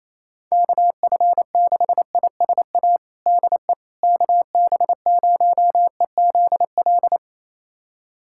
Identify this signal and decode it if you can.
Morse code, heard as a keyed tone, about 28 wpm, 700 Hz. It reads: KF6ISA DE KB0EZL
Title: USA Callsign DE Callsign - 6 Characters - 28wpm